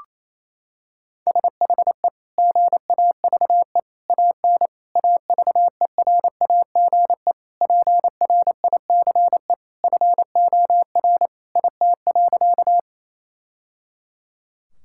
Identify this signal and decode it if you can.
Morse code, heard as a keyed tone, about 28 wpm, 700 Hz. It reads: SHE GAVE AN AVERAGE PRICE FOR IT.
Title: Sentences from Top 1000 Words - Review Part 6 - 28wpm